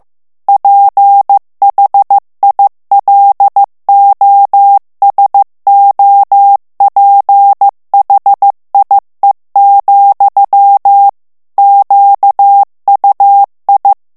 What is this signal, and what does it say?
Morse code, heard as a keyed tone, about 15 wpm, 800 Hz. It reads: PHILOSOPHIE, QUI
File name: FRANCAIS TEXTE CLAIR 2 VITESSE 14 MOTS MINUTES